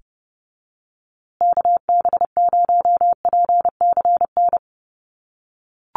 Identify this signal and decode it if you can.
Morse code, heard as a keyed tone, about 30 wpm, 700 Hz. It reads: KB0PCN
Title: USA Call Signs Repeated 2x 30wpm